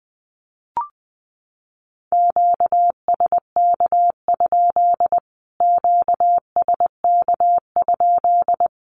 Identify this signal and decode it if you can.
Morse code, heard as a keyed tone, about 20 wpm, 700 Hz. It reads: QSK? QSK?
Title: Q-Codes Repeated 3x 20wpm